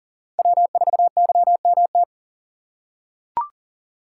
Morse code, heard as a keyed tone, about 40 wpm, 700 Hz. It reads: W4YMT
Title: USA Call Signs - 3x Character Spacing - 40wpm